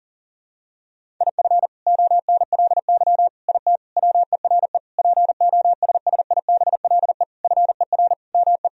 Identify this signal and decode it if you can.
Morse code, heard as a keyed tone, about 40 wpm, 700 Hz. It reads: IF ONLY IT WERE POSSIBLE FER ME
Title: Sentences from Top 500 Words and Abbreviations 40wpm